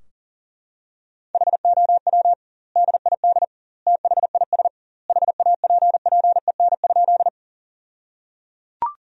Morse code, heard as a keyed tone, about 37 wpm, 700 Hz. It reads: HOW DID THIS HAPPEN?
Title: 2x Word-Spacing - Sentences from Top 400 Words - 40wpm